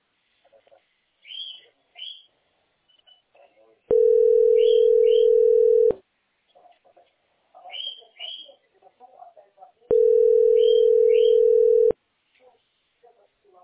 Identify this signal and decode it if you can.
Ringback tone (North America), 440 + 480 Hz, with speech.
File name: 0f -73cd-4cdc-8e25